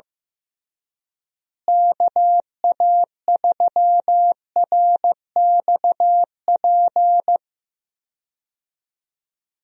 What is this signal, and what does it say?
Morse code, heard as a keyed tone, about 16 wpm, 700 Hz. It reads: KA3RXP
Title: USA Call Signs - 6 Characters - Repeated 3x - 15wpm